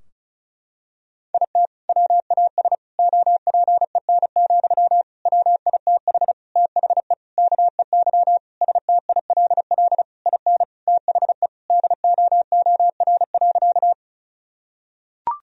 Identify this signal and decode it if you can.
Morse code, heard as a keyed tone, about 36 wpm, 700 Hz. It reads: IT WAS OPEN, WITH THE KEY STILL IN THE DOOR.
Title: Sentences from Top 1000 Words - Review Part 4 - 35wpm